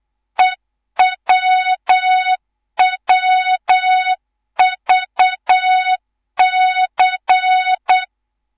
Morse code, heard as a keyed tone, about 8 wpm, 3700 Hz, with noise. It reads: EWWVC
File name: Morze-HA2MN / hang / 11 GYAKORLAT